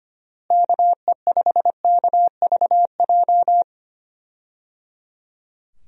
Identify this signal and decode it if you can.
Morse code, heard as a keyed tone, about 25 wpm, 700 Hz. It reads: KE5KVJ
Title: USA Call Signs - 6 Characters - Repeated 3x - 25wpm